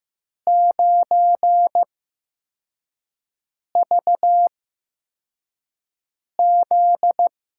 Morse code, heard as a keyed tone, about 15 wpm, spaced as for 4 wpm, 700 Hz. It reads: 9VZ
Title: USA Call Signs - 5 Characters - 8x Character Spacing - 15wpm